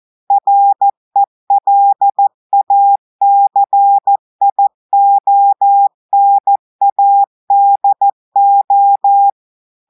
Morse code, heard as a keyed tone, about 14 wpm, 800 Hz. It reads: RELACIONADO
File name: audios / portugais / portugais texte en clair 2 vitesse 14 mots minutes